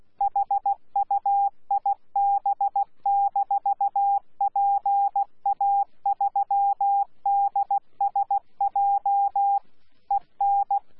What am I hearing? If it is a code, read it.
Morse code, heard as a keyed tone, about 16 wpm, 800 Hz. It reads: HUIB-PA3DSJ EN
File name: Z PI4VRZ / PI4VRZ 14-jun-2025